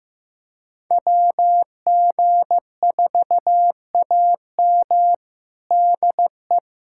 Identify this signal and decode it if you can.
Morse code, heard as a keyed tone, about 15 wpm, 700 Hz. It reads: WG4AM DE
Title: USA Callsign DE Callsign 15wpm